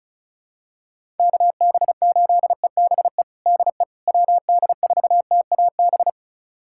Morse code, heard as a keyed tone, about 34 wpm, 700 Hz. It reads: KB8EBE DE WD4TAB
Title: USA Callsign DE Callsign - 6 Characters - 35wpm